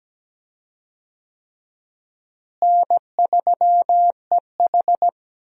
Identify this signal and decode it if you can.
Morse code, heard as a keyed tone, about 17 wpm, 700 Hz. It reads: N3EH
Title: USA Call Signs - 4 Characters - Repeated 3x - 17wpm